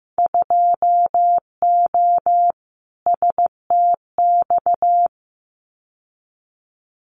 Morse code, heard as a keyed tone, about 15 wpm, 700 Hz. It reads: 2O STX
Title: Winter Field Day Exchanges 15wpm